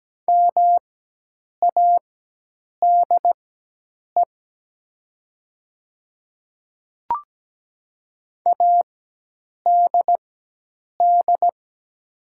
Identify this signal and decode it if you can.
Morse code, heard as a keyed tone, about 17 wpm, spaced as for 8 wpm, 700 Hz. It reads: MADE ADD
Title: 4x Character Spacing - Top 500 Words - 17wpm